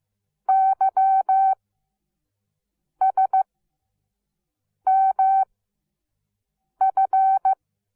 Morse code, heard as a keyed tone, about 15 wpm, 770 Hz. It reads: Y S M F
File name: Onlinelehrgang / morsen / DJ4UF Morsen17